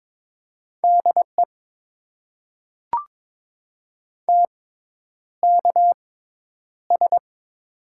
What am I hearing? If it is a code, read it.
Morse code, heard as a keyed tone, about 22 wpm, spaced as for 10 wpm, 700 Hz. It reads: DE TKS
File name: Top 90 QSO Elements - 6x Character Spacing - 22wpm